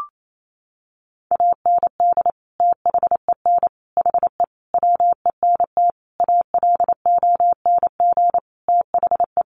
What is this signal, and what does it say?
Morse code, heard as a keyed tone, about 28 wpm, 700 Hz. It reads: AND THEN HE WENT ALONG THE
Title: Sentences from Top 1100 Words 28wpm